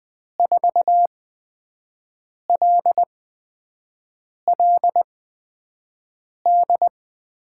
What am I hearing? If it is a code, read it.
Morse code, heard as a keyed tone, about 20 wpm, spaced as for 5 wpm, 700 Hz. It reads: 4LLD
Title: USA Call Signs - 8x Character Spacing - 20wpm